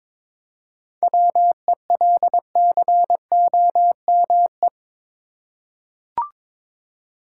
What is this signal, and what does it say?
Morse code, heard as a keyed tone, about 22 wpm, 700 Hz. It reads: WELCOME
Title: Seven Letter Words - 22wpm